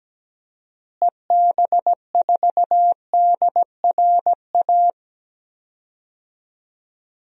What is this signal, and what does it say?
Morse code, heard as a keyed tone, about 17 wpm, 700 Hz. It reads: EB4DRA
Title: DXCC-100 Call Signs - 6 Characters - Repeated 3x - 17wpm